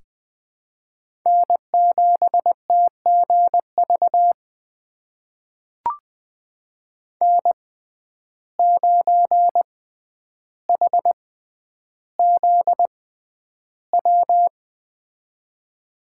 Morse code, heard as a keyed tone, about 20 wpm, spaced as for 10 wpm, 700 Hz. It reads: N7TGV N9HZW
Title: USA Call Signs - 5 Characters - 6x Character Spacing - 20wpm